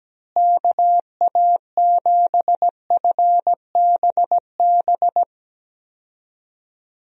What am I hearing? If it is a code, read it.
Morse code, heard as a keyed tone, about 17 wpm, 700 Hz. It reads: KA7FBB
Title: USA Call Signs - 6 Characters - Repeated 2x - 17wpm